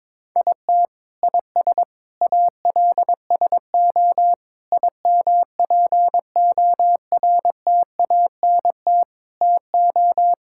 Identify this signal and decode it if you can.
Morse code, heard as a keyed tone, about 22 wpm, 700 Hz. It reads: IT IS ALSO IMPORTANT TO
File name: practice / 22wpm / Sets of 8 Words - 22wpm